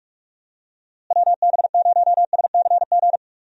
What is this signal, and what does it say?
Morse code, heard as a keyed tone, about 44 wpm, 700 Hz. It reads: WB0SCG
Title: USA Call Signs - 6 Characters - Repeated 2x - 45wpm